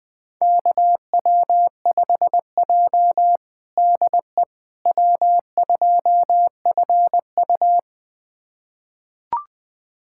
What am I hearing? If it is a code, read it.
Morse code, heard as a keyed tone, about 20 wpm, 700 Hz. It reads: KW5J DE W2FU
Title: USA Callsign DE Callsign - 4 Characters - 20wpm